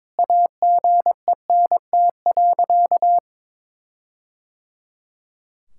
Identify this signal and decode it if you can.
Morse code, heard as a keyed tone, about 22 wpm, 700 Hz. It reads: AGENT.
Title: Sentences from Top 1600 Words 22wpm